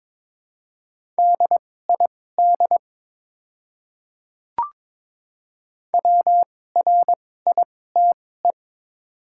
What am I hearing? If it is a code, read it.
Morse code, heard as a keyed tone, about 22 wpm, spaced as for 16 wpm, 700 Hz. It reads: DID WRITE
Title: 2x Character Spacing - Top 100 Words - 22wpm